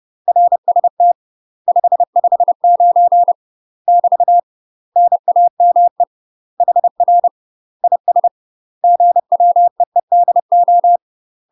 Morse code, heard as a keyed tone, about 30 wpm, 700 Hz. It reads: RST 559 <BT> NAME HR IS GWEEDO
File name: UK MASTER QSOs TRAINING / 30WPM / UK MASTER QSOs 0047